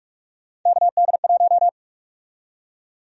Morse code, heard as a keyed tone, about 47 wpm, 700 Hz. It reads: KD1
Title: USA Call Sign Prefixes - 45wpm